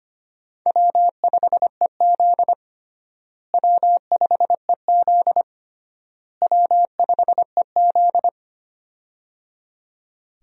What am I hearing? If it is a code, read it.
Morse code, heard as a keyed tone, about 25 wpm, 700 Hz. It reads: W5EZ W5EZ W5EZ
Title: USA Call Signs - 4 Characters - Repeated 3x - 25wpm